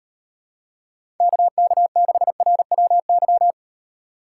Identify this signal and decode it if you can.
Morse code, heard as a keyed tone, about 38 wpm, 700 Hz. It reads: KK6RWY